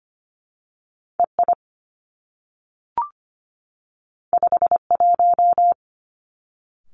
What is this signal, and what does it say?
Morse code, heard as a keyed tone, about 24 wpm, 700 Hz. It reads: E I 5 1
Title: Two Challenging Letter-Number Combinations 25wpm